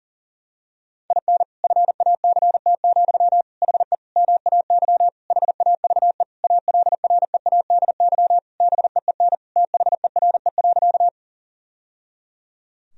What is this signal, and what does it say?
Morse code, heard as a keyed tone, about 40 wpm, 700 Hz. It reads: IN FACT, HE MAY HAVE ALREADY BEEN THERE.
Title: Sentences from Top 1000 Words - Review Part 3 - 40wpm